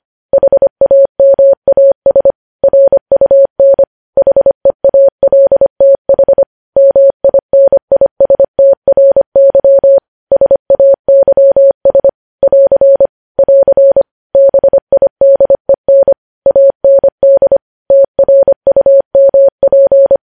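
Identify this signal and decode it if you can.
Morse code, heard as a keyed tone, about 25 wpm, 550 Hz. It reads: HAMAS RUN HEALTH MINISTRY SAYS <AR> <AR> BIDEN AND TRUMP
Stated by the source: Morse news headlines sent at 25 wpm.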